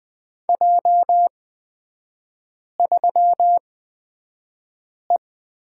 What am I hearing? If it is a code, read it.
Morse code, heard as a keyed tone, about 20 wpm, spaced as for 5 wpm, 700 Hz. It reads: J3E
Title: 243 Lesson 39e - Letter z - Three Characters 20wpm